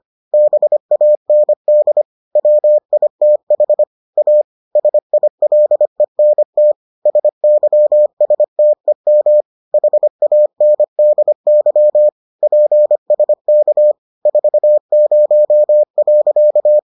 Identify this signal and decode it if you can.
Morse code, heard as a keyed tone, about 25 wpm, 600 Hz. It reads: BAND WITH A SILENT SYSTEM HANDY PSK 40.